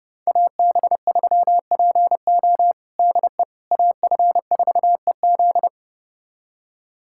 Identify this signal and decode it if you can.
Morse code, heard as a keyed tone, about 30 wpm, 700 Hz. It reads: AB3PO DE AF4EZ
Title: USA Callsign DE Callsign - 5 Characters - 30wpm